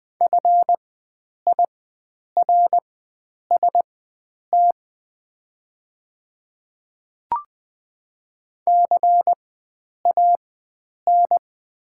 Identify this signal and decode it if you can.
Morse code, heard as a keyed tone, about 20 wpm, spaced as for 9 wpm, 690 Hz. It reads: FIRST CAN
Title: 4x Character Spacing - Top 100 Words - 20wpm